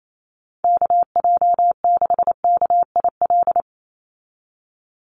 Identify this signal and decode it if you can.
Morse code, heard as a keyed tone, about 28 wpm, 700 Hz. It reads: KJ6KIL
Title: USA Call Signs 28wpm